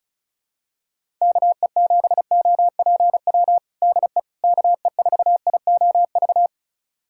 Morse code, heard as a keyed tone, about 34 wpm, 700 Hz. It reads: KE7OPW DE KE4IOV
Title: USA Callsign DE Callsign - 6 Characters - 35wpm